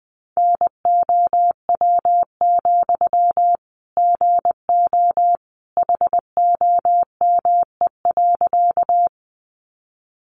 Morse code, heard as a keyed tone, about 19 wpm, 700 Hz. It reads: NOW, GO HOME.
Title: Sentences from Top 200 Words 20wpm